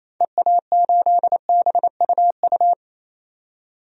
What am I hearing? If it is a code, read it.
Morse code, heard as a keyed tone, about 27 wpm, 700 Hz. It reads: EA8BUU
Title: DXCC-100 Call Signs - 6 Characters - 28wpm